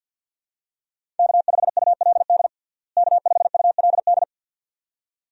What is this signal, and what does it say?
Morse code, heard as a keyed tone, about 52 wpm, 700 Hz. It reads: K5ULD K5ULD
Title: USA Call Signs - 5 Characters - Repeated 2x - 50wpm